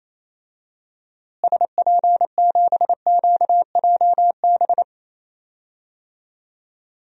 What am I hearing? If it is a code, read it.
Morse code, heard as a keyed tone, about 28 wpm, 700 Hz. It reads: SP7QJB
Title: DXCC-100 Call Signs - 6 Characters - Repeated 3x - 28wpm